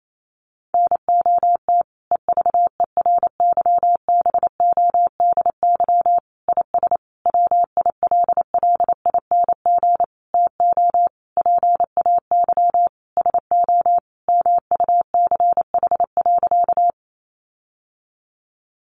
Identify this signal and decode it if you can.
Morse code, heard as a keyed tone, about 28 wpm, 700 Hz. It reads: NOT EVERYBODY IS WILLING TO PAY SO MUCH.
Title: Sentences from Top 1400 Words 28wpm